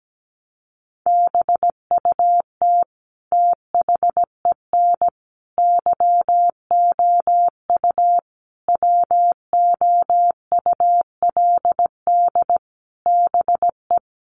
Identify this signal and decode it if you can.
Morse code, heard as a keyed tone, about 17 wpm, 700 Hz. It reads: BUT THEN YOU WOULD BE
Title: Sentences from Top 300 Words 17wpm